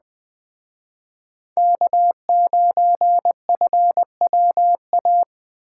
Morse code, heard as a keyed tone, about 20 wpm, 700 Hz. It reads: K9FWA